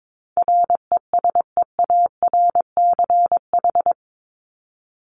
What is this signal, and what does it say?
Morse code, heard as a keyed tone, about 22 wpm, 700 Hz. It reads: RESEARCH
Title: Top 501-750 Words 22wpm